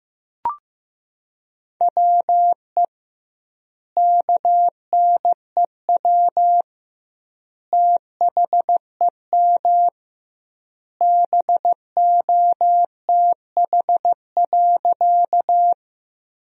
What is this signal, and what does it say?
Morse code, heard as a keyed tone, about 15 wpm, 700 Hz. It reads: WE KNEW THEM BOTH.
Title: 2x Word-Spacing - Sentences from Top 400 Words - 15wpm